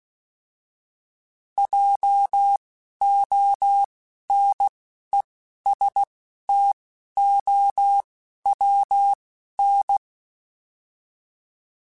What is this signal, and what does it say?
Morse code, heard as a keyed tone, about 16 wpm, spaced as for 11 wpm, 800 Hz. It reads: JONESTOWN